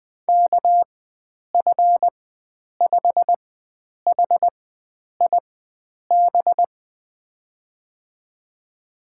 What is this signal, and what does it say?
Morse code, heard as a keyed tone, about 21 wpm, spaced as for 9 wpm, 700 Hz. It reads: KF5HIB
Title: USA Call Signs - 6 Characters - 4x Character Spacing - 20wpm